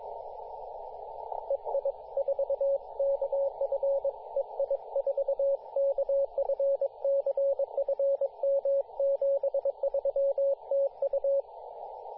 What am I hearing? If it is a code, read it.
Morse code, heard as a keyed tone, about 22 wpm, 550 Hz, with radio noise. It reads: EI4KF EI4KF CFM 73 TU
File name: posts / 2017-sep-22-0800z-jd1bon-14-040mhz-cw